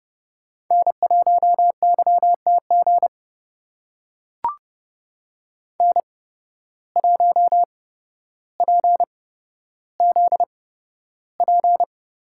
Morse code, heard as a keyed tone, about 30 wpm, spaced as for 13 wpm, 700 Hz. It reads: N1YTG N1PZP
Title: USA Call Signs - 5 Characters - 8x Character Spacing - 30wpm